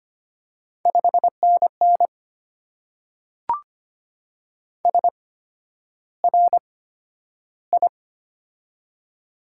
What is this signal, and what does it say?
Morse code, heard as a keyed tone, about 25 wpm, spaced as for 11 wpm, 700 Hz. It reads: 5NN SRI
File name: Top 90 QSO Elements - 8x Character Spacing - 25wpm